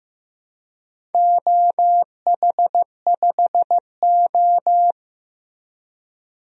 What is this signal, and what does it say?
Morse code, heard as a keyed tone, about 15 wpm, 700 Hz. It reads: OH5O
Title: DXCC-100 Call Signs - 4 Characters - Repeated 3x - 15wpm